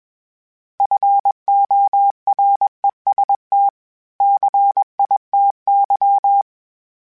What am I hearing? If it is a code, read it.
Morse code, heard as a keyed tone, about 20 wpm, 800 Hz. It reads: FOREST CITY
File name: fromSMs / FromSMs mp3s / Cities@20